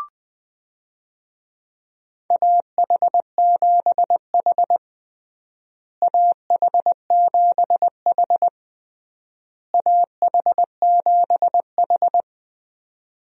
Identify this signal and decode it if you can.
Morse code, heard as a keyed tone, about 20 wpm, 700 Hz. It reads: AH7H AH7H AH7H
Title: USA Call Signs - 4 Characters - Repeated 3x - 20wpm